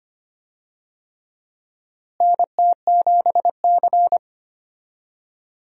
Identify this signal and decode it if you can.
Morse code, heard as a keyed tone, about 25 wpm, 700 Hz. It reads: NT7C
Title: USA Call Signs - 4 Characters - Repeated 3x - 25wpm